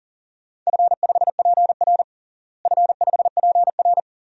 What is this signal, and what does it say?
Morse code, heard as a keyed tone, about 37 wpm, 700 Hz. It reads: F5PR F5PR
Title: DXCC-100 Call Signs - 4 Characters - Repeated 3x - 40wpm